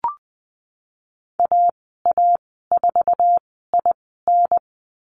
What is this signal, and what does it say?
Morse code, heard as a keyed tone, about 20 wpm, spaced as for 15 wpm, 700 Hz. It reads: AA4IN
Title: 048 Lesson 9g - Number 4 - Call Signs 20wpm